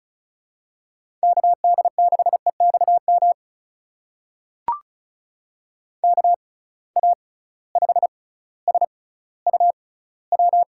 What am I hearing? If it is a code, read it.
Morse code, heard as a keyed tone, about 36 wpm, spaced as for 18 wpm, 700 Hz. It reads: KD6EXM KA5SUW
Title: USA Call Signs - 6 Characters - 6x Character Spacing - 35wpm